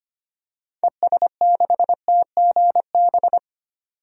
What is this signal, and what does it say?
Morse code, heard as a keyed tone, about 25 wpm, 700 Hz. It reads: ES6TGB
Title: DXCC-100 Call Signs - 6 Characters - 25wpm